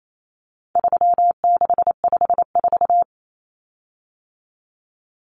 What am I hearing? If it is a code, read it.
Morse code, heard as a keyed tone, about 28 wpm, 700 Hz. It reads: 3654
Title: Common Cut Numbers - 4-Digit 28wpm